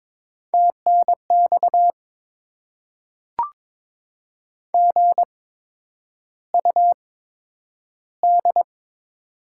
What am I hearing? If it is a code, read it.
Morse code, heard as a keyed tone, about 22 wpm, spaced as for 9 wpm, 700 Hz. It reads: TNX GUD
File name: Top 20 QSO Elements - 8x Character Spacing - 22wpm